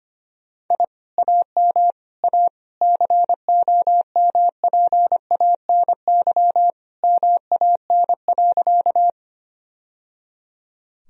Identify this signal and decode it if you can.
Morse code, heard as a keyed tone, about 25 wpm, 700 Hz. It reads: I AM A COMPANY MAN.
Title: Sentences from Top 1000 Words - Review Part 3 - 25wpm